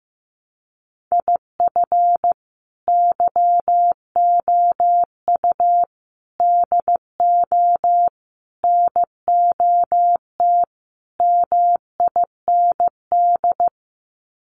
Morse code, heard as a keyed tone, about 15 wpm, 700 Hz. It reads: IF YOU DO NOT MIND
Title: Sets of 5 Words 15wpm